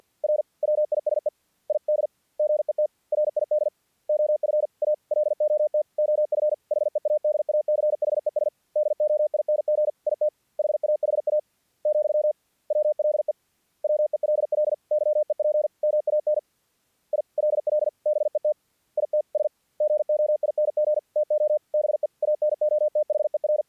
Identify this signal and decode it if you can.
Morse code, heard as a keyed tone, about 48 wpm, 600 Hz. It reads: R WISE ID GET RID OF A LOT OF HEADACHES DOING IT HAHA, WLE WELL YEP MAN ILL BET ITS GOING TO BE ANOTHER
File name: archived / CRSnet